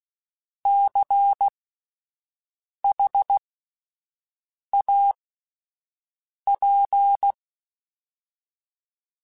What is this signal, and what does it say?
Morse code, heard as a keyed tone, about 16 wpm, spaced as for 5 wpm, 800 Hz. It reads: CHAP
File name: Spaced Alphas / Spaced Alphas mp3s / Cs sp @10